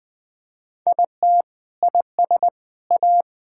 Morse code, heard as a keyed tone, about 21 wpm, 700 Hz. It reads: IT IS A
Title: Sentences from Top 1000 Words - Review Part 6 - 20wpm